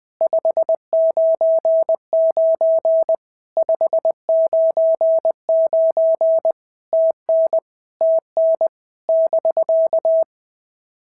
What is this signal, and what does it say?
Morse code, heard as a keyed tone, about 20 wpm, 650 Hz. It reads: 599 599 TN TN <BK>
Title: POTA QSO Practice - 20wpm